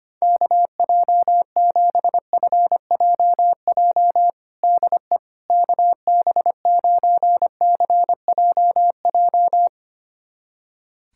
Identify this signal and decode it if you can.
Morse code, heard as a keyed tone, about 25 wpm, 700 Hz. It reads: KJ7FJJ DE KB9CJJ
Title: USA Callsign DE Callsign - 6 Characters - 25wpm